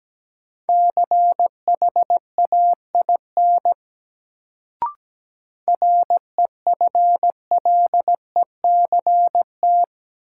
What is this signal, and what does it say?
Morse code, heard as a keyed tone, about 17 wpm, 700 Hz. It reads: CHAIN REFLECT
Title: Sentences from Top 1500 Words - New Word Drill Alt - 17wpm